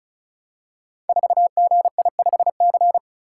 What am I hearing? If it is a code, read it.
Morse code, heard as a keyed tone, about 36 wpm, 690 Hz. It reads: 4GI5C
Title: Code Groups - 5 Letter:Numbers - 35wpm